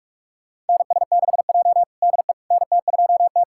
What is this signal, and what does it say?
Morse code, heard as a keyed tone, about 46 wpm, 700 Hz. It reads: NS6J DE NT2T
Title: USA Callsign DE Callsign - 4 Characters - 45wpm